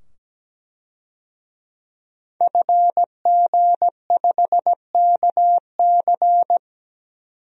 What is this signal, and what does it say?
Morse code, heard as a keyed tone, about 17 wpm, 700 Hz. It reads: FG5KC
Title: DXCC-100 Call Signs - 5 Characters - 17wpm